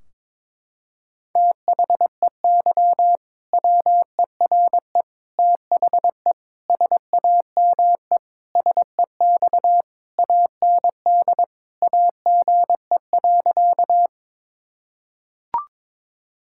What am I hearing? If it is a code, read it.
Morse code, heard as a keyed tone, about 22 wpm, 700 Hz. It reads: THEY WERE THE SAME SEX AND AGE.
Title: Sentences from Top 1600 Words 22wpm